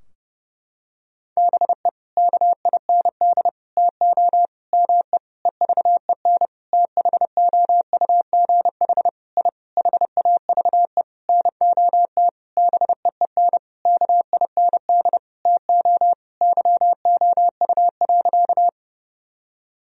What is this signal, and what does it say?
Morse code, heard as a keyed tone, about 30 wpm, 700 Hz. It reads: BE KIND TO ME EVEN THOUGH I HAVE NOT BEEN KIND TO YOU.
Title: Sentences from Top 1000 Words - Review Part 2 - 30wpm